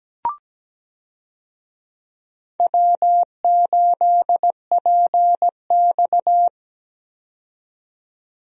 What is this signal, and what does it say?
Morse code, heard as a keyed tone, about 17 wpm, 700 Hz. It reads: W8PX
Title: USA Call Signs - 4 Characters - 17wpm